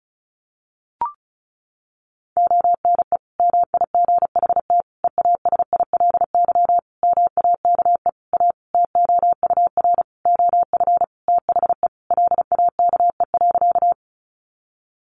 Morse code, heard as a keyed tone, about 34 wpm, 700 Hz. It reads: ONE MIGHT EASILY MAKE A TOUR OF THE LAKE.
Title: Sentences from Top 1400 Words 35wpm